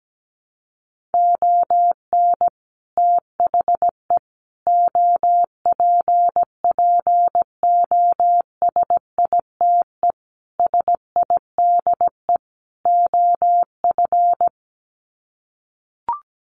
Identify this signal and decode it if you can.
Morse code, heard as a keyed tone, about 17 wpm, 700 Hz. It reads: ON THE OPPOSITE SIDE OF
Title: Sets of 5 Words Encore2 17wpm